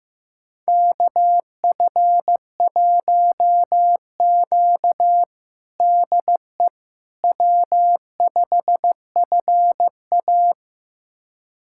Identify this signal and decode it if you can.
Morse code, heard as a keyed tone, about 15 wpm, 700 Hz. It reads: KF1Q DE W5FA
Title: USA Callsign DE Callsign - 4 Characters - 15wpm